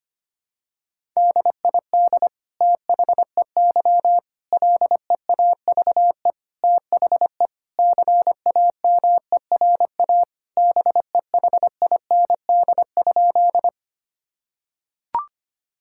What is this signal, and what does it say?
Morse code, heard as a keyed tone, about 25 wpm, 700 Hz. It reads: DID THEY LEAVE THE CAMERA BEHIND?
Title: Sentences from Top 1000 Words - Review Part 5 - 25wpm